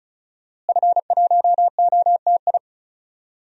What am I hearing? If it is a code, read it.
Morse code, heard as a keyed tone, about 36 wpm, 700 Hz. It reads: F1OTI
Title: DXCC-100 Call Signs - 5 Characters - Repeated 2x - 35wpm